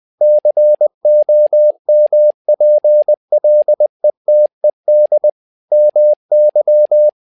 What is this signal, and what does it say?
Morse code, heard as a keyed tone, about 20 wpm, 600 Hz. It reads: COMPLETED MY
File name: learning center / tower trouble / tower trouble 20wpm